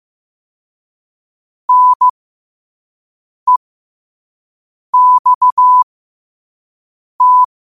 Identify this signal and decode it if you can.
Morse code, heard as a keyed tone, about 15 wpm, spaced as for 5 wpm, 1000 Hz. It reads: NEXT
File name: qotdfiles / 5wpm